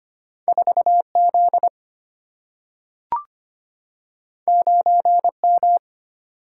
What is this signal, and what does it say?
Morse code, heard as a keyed tone, about 25 wpm, 700 Hz. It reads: 4 Z 9 M
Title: Single-Letter Single-Number 25wpm